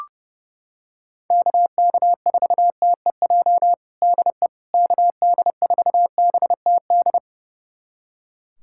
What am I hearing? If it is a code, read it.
Morse code, heard as a keyed tone, about 30 wpm, 700 Hz. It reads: KK4TEJ DE KD4BTD
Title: USA Callsign DE Callsign - 6 Characters - 30wpm